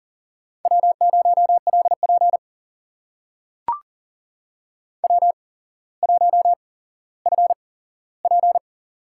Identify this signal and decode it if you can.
Morse code, heard as a keyed tone, about 40 wpm, spaced as for 17 wpm, 700 Hz. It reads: W0LP W1FP
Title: USA Call Signs - 4 Characters - 8x Character Spacing - 40wpm